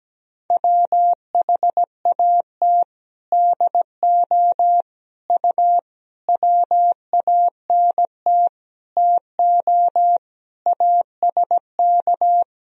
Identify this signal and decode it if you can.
Morse code, heard as a keyed tone, about 17 wpm, 700 Hz. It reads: WHAT DO U WANT TO ASK
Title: Sentences from Top 200 Words and Abbreviations 17wpm